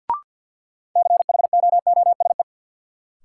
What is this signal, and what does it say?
Morse code, heard as a keyed tone, about 51 wpm, 700 Hz. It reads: CHOOSE
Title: Sentences from Top 700 Words - New Word Drill - 50wpm